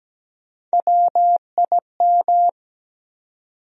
Morse code, heard as a keyed tone, about 17 wpm, 700 Hz. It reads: WIM
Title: USA Call Sign Postfixes - 3 Letters 17wpm